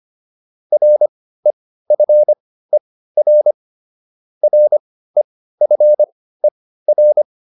Morse code, heard as a keyed tone, about 25 wpm, spaced as for 15 wpm, 600 Hz. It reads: REFER REFER